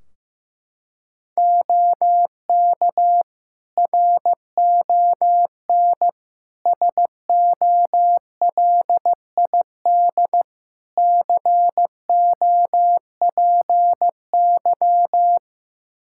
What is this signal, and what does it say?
Morse code, heard as a keyed tone, about 16 wpm, 700 Hz. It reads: OK RON SOLID COPY
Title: Practice QSO - Exchange 3 - More Info - 15wpm